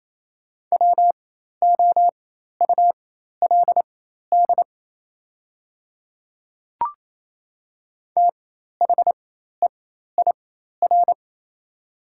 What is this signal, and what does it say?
Morse code, heard as a keyed tone, about 28 wpm, spaced as for 13 wpm, 700 Hz. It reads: WOULD THEIR
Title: 4x Character Spacing - Top 100 Words - 28wpm